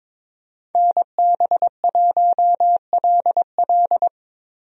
Morse code, heard as a keyed tone, about 21 wpm, 700 Hz. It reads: NB1LL
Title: USA Call Signs - 5 Characters - Repeated 3x - 22wpm